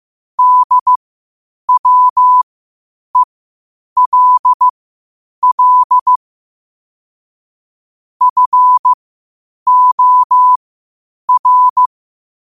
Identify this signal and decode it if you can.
Morse code, heard as a keyed tone, about 15 wpm, spaced as for 8 wpm, 1000 Hz. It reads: DWELL FOR
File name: qotdfiles / 8wpm